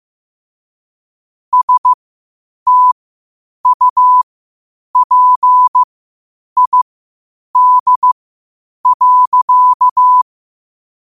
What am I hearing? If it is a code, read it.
Morse code, heard as a keyed tone, about 15 wpm, spaced as for 8 wpm, 1000 Hz. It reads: STUPID.
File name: qotdfiles / 8wpm